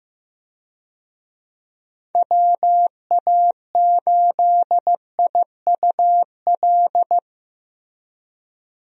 Morse code, heard as a keyed tone, about 15 wpm, 700 Hz. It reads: WA8IUL
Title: USA Call Signs - 6 Characters - Repeated 3x - 15wpm